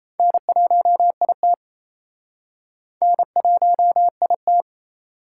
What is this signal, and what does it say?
Morse code, heard as a keyed tone, about 37 wpm, 700 Hz. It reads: N1IT N1IT